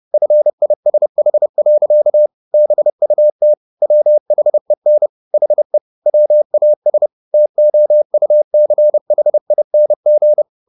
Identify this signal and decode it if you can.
Morse code, heard as a keyed tone, about 30 wpm, 600 Hz. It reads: FISH. BUT WHEN HE WAS TOUCHING
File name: the old man and the sea / the old man and the sea 1 30wpm